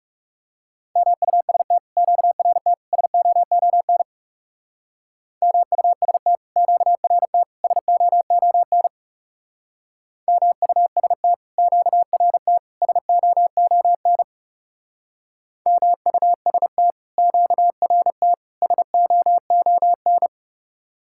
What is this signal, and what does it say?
Morse code, heard as a keyed tone, about 35 wpm, 700 Hz. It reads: MUST QRT SOON MUST QRT SOON MUST QRT SOON MUST QRT SOON
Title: Speed-Racing Common QSO Phrases - 30wpm